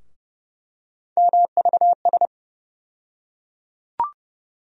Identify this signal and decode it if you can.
Morse code, heard as a keyed tone, about 30 wpm, 700 Hz. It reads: MVS
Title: USA Call Sign Postfixes - 3 Letters 30wpm